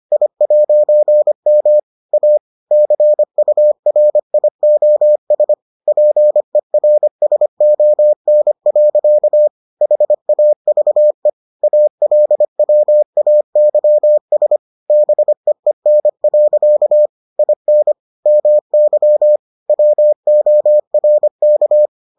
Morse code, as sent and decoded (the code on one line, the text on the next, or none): .. .----. -- / .- / -.-. ..- .-. .. --- ... / .--. .-. ... --- -. .-.-.- / .... .- ...- . / .- .-.. .-- .- -.-- ... / -... . -. .-.-.- / .. -. / -- -.-- / .-- --- .-. -.-
I'M A CURIOS PERSON. HAVE ALWAYS BEEN. IN MY WORK